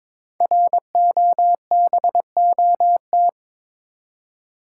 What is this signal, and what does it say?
Morse code, heard as a keyed tone, about 22 wpm, 700 Hz. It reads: ROBOT
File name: 5 Letter Words - First 3 Letters Given - 22wpm